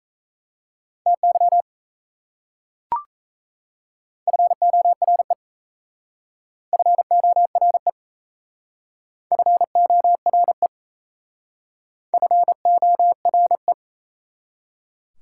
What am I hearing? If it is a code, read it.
Morse code, heard as a keyed tone, about 34 wpm, 690 Hz. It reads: TY FORE FORE FORE FORE
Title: Speed-Racing Prefixes and Suffixes 28wpm